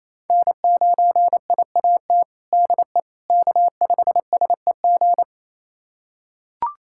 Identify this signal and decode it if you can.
Morse code, heard as a keyed tone, about 28 wpm, 700 Hz. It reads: N9IAT DE K5SEG
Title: USA Callsign DE Callsign - 5 Characters - 28wpm